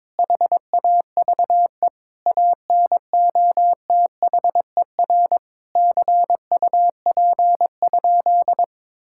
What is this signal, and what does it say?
Morse code, heard as a keyed tone, about 22 wpm, 700 Hz. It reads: HAVE ANOTHER CUP?
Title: Sentences from Top 1600 Words 22wpm